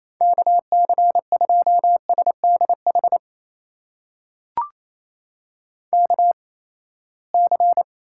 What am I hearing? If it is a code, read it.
Morse code, heard as a keyed tone, about 28 wpm, spaced as for 19 wpm, 700 Hz. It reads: KC2SDH KC
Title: USA Call Signs - 8x Character Spacing - 28wpm